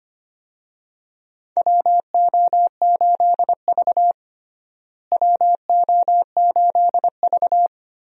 Morse code, heard as a keyed tone, about 25 wpm, 700 Hz. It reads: WO8V WO8V
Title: USA Call Signs - 4 Characters - Repeated 2x - 25wpm